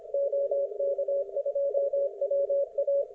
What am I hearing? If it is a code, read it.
Morse code, heard as a keyed tone, about 26 wpm, 560 Hz. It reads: OK2WA